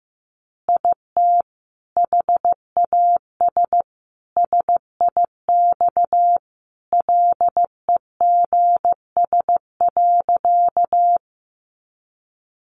Morse code, heard as a keyed tone, about 15 wpm, 700 Hz. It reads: IT HAS SIX LEGS.
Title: Additional Easy Phrases 15wpm